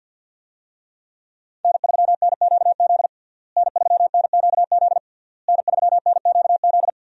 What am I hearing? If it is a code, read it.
Morse code, heard as a keyed tone, about 52 wpm, 700 Hz. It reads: N3NQZ N3NQZ N3NQZ
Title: USA Call Signs - 5 Characters - Repeated 3x - 50wpm